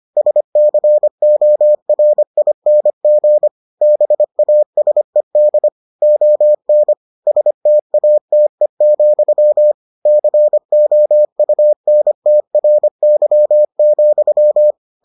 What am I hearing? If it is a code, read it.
Morse code, heard as a keyed tone, about 25 wpm, 600 Hz. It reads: SCORING BASED ON STATE, COUNTRY,